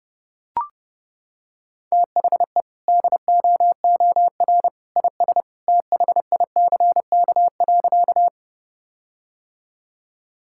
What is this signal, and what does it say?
Morse code, heard as a keyed tone, about 32 wpm, 700 Hz. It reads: THE DOOR IS THICK.
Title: Sentences from Top 1600 Words 30wpm